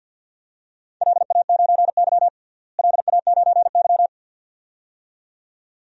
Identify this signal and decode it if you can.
Morse code, heard as a keyed tone, about 52 wpm, 700 Hz. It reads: LA9Y LA9Y
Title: DXCC-100 Call Signs - 4 Characters - Repeated 2x - 50wpm